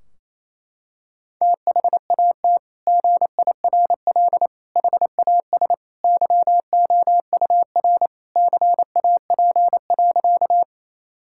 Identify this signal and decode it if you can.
Morse code, heard as a keyed tone, about 27 wpm, 700 Hz. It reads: THAT GIRL HAS YOUR CAP.
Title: Sentences from Top 1600 Words 28wpm